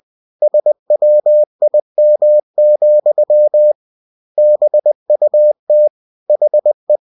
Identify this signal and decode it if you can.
Morse code, heard as a keyed tone, about 20 wpm, 600 Hz. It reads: SWIM, BUT HE
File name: cw-books / winnie-the-pooh / winnie-20wpm-page-0103